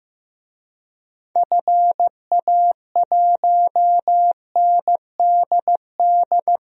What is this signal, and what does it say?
Morse code, heard as a keyed tone, about 15 wpm, 700 Hz. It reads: FA1NDD
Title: DXCC-100 Call Signs - 6 Characters - 15wpm